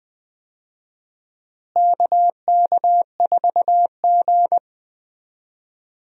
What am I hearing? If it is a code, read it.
Morse code, heard as a keyed tone, about 20 wpm, 700 Hz. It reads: KK4G
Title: USA Call Signs - 4 Characters - Repeated 3x - 20wpm